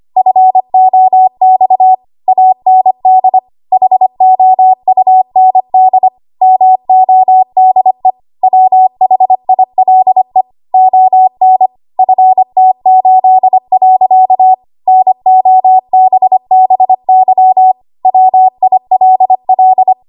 Morse code, heard as a keyed tone, about 25 wpm, 750 Hz. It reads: FOX AND HOUND MODE WHILE ON FT8. NOBBY WILL
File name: Archive / 25 WPM / 25WPM